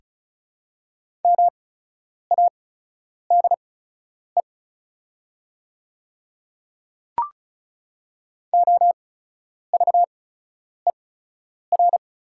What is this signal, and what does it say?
Morse code, heard as a keyed tone, about 36 wpm, spaced as for 10 wpm, 700 Hz. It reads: MADE OVER
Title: 8x Character Spacing - Top 300 Words - 35wpm